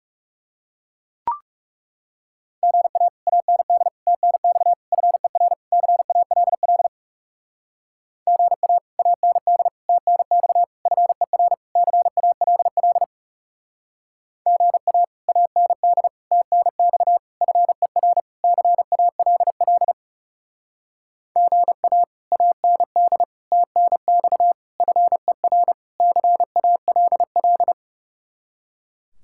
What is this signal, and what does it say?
Morse code, heard as a keyed tone, about 40 wpm, 700 Hz. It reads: GA AND TNX FER CALL GA AND TNX FER CALL GA AND TNX FER CALL GA AND TNX FER CALL